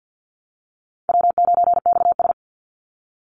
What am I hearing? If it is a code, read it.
Morse code, heard as a keyed tone, about 49 wpm, 700 Hz. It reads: W9XS
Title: Call Signs Repeated 2x - 50wpm